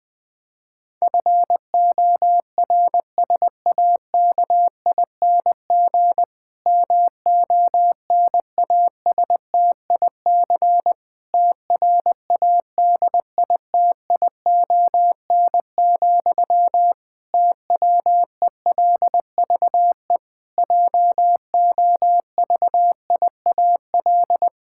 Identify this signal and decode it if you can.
Morse code, heard as a keyed tone, about 19 wpm, 700 Hz. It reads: FORSAKING MONASTIC TRADITION, TWELVE JOVIAL
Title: Pangram Challenge v2 - 20wpm